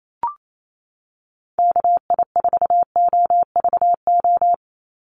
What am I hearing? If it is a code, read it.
Morse code, heard as a keyed tone, about 28 wpm, 700 Hz. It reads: KI4OVO
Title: USA Call Signs 28wpm